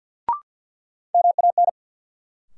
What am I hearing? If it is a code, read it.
Morse code, heard as a keyed tone, about 50 wpm, 700 Hz. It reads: MAN
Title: Sentences from Top 200 Words - New Word Drill - 50wpm